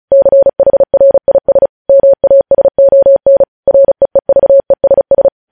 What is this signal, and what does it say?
Morse code, heard as a keyed tone, about 36 wpm, 550 Hz. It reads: CHRIS MASON REEVESS
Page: news headlines in morse code at 35 wpm